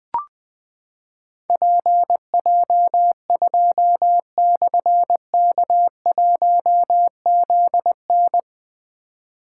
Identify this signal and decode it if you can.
Morse code, heard as a keyed tone, about 20 wpm, 700 Hz. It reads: PJ2/K1ZN
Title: Difficult Call Signs 20wpm